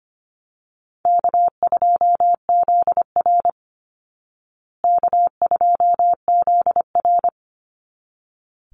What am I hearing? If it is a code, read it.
Morse code, heard as a keyed tone, about 25 wpm, 700 Hz. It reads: K2ZR K2ZR
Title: Easy Call Signs Repeated 2x - 25wpm